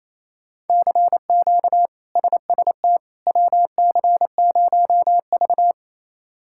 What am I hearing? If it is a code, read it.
Morse code, heard as a keyed tone, about 28 wpm, 700 Hz. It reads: CQ SST WC0V
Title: K1USN SST QSO Practice - 28wpm